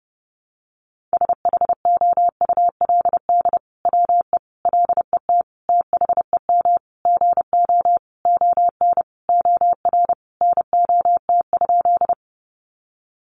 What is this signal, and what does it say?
Morse code, heard as a keyed tone, about 30 wpm, 700 Hz. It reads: SHOULD WE LET THEM GO ON OR NOT?
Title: Sentences from Top 600 Words 30wpm